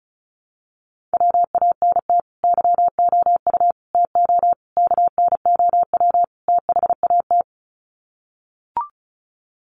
Morse code, heard as a keyed tone, about 34 wpm, 700 Hz. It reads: WANT YOU TO KNOW THAT
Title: Sets of 5 Words Encore 35wpm